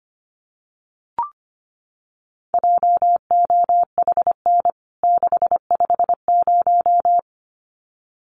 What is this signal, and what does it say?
Morse code, heard as a keyed tone, about 25 wpm, 700 Hz. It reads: JOHN 650
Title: CWT Exchange 25wpm